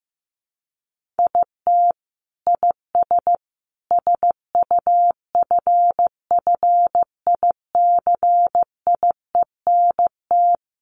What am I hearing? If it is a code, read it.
Morse code, heard as a keyed tone, about 15 wpm, 700 Hz. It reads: IT IS SUFFICIENT
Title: Sentences from Top 1400 Words 15wpm